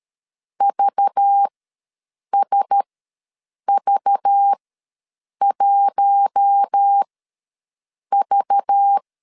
Morse code, heard as a keyed tone, about 13 wpm, spaced as for 7 wpm, 800 Hz. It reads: VSV1V
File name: LEZIONI / 5 LBVC / 5 LBV ws 12wpm 3015